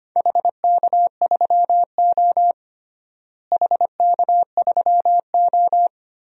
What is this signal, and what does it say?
Morse code, heard as a keyed tone, about 25 wpm, 700 Hz. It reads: HK3O HK3O
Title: DXCC-100 Call Signs - 4 Characters - Repeated 3x - 25wpm